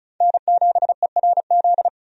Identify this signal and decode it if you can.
Morse code, heard as a keyed tone, about 36 wpm, 700 Hz. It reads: N7ERZ
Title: USA Call Signs - 5 Characters - 35wpm